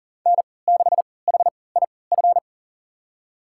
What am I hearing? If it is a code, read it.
Morse code, heard as a keyed tone, about 40 wpm, spaced as for 23 wpm, 700 Hz. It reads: N6HIF
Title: USA Call Signs - 5 Characters - 3x Character Spacing - 40wpm